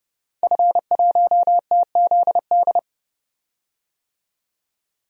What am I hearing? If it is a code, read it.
Morse code, heard as a keyed tone, about 32 wpm, 700 Hz. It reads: F1TZD
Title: DXCC-100 Call Signs - 5 Characters - 30wpm